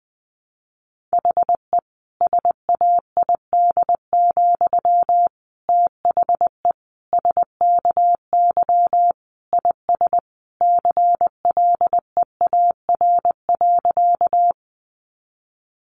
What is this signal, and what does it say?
Morse code, heard as a keyed tone, about 20 wpm, 700 Hz. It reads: HE SAID, THE SKY IS CLEAR.
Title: Sentences from Top 1200 Words 20wpm